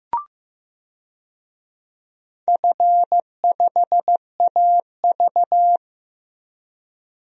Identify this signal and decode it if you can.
Morse code, heard as a keyed tone, about 16 wpm, 700 Hz. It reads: F5AV
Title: DXCC-100 Call Signs - 4 Characters - Repeated 2x - 15wpm